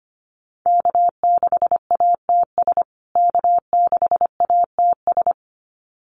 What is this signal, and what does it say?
Morse code, heard as a keyed tone, about 25 wpm, 700 Hz. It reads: K6ATS K6ATS
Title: USA Call Signs Repeated 2x 25wpm